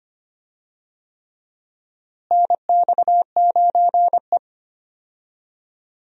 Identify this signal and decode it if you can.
Morse code, heard as a keyed tone, about 25 wpm, 700 Hz. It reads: NX9E
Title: USA Call Signs - 4 Characters - Repeated 3x - 25wpm